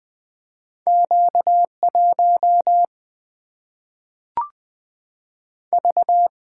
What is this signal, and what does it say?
Morse code, heard as a keyed tone, about 20 wpm, 700 Hz. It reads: Q1 V
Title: Single-Letter Single-Number 20wpm